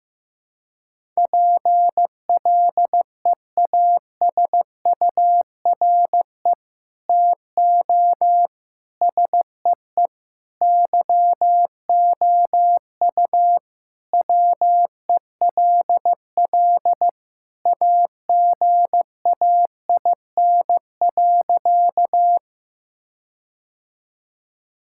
Morse code, heard as a keyed tone, about 15 wpm, 700 Hz. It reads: PLEASURE TO SEE YOU WELL AGAIN.
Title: Sentences from Top 1600 Words 15wpm